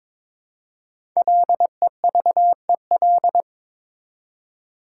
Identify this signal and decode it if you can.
Morse code, heard as a keyed tone, about 22 wpm, 700 Hz. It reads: LEVEL
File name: Difficult Words with Hints - 22wpm